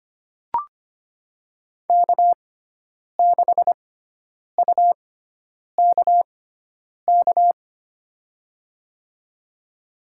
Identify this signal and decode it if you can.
Morse code, heard as a keyed tone, about 25 wpm, spaced as for 9 wpm, 700 Hz. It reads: K6UKK
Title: USA Call Signs - 6x Character Spacing - 25wpm